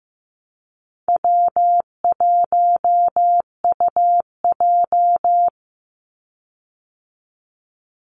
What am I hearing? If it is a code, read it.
Morse code, heard as a keyed tone, about 15 wpm, 700 Hz. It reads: W1UJ
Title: Call Signs Repeated 3x - 15wpm